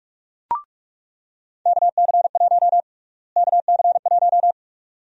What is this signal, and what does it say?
Morse code, heard as a keyed tone, about 46 wpm, 700 Hz. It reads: KC1 KC1
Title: USA Call Sign Prefixes - Repeated 2x - 45wpm